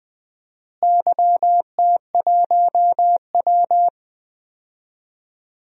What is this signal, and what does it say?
Morse code, heard as a keyed tone, about 20 wpm, 700 Hz. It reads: YT1W
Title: DXCC-100 Call Signs - 4 Characters - Repeated 2x - 20wpm